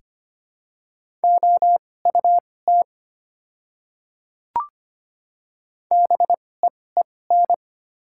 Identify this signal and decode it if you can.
Morse code, heard as a keyed tone, about 25 wpm, spaced as for 18 wpm, 700 Hz. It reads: OUT BEEN
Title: 2x Character Spacing - Top 300 Words - 25wpm